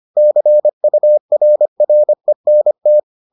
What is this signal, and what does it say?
Morse code, heard as a keyed tone, about 25 wpm, 600 Hz. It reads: CURRENT